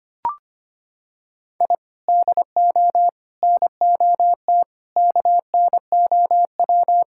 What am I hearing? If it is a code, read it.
Morse code, heard as a keyed tone, about 25 wpm, 700 Hz. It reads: I DO NOT KNOW
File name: practice / 25wpm / Sentences with 3-Word Phrases - 25wpm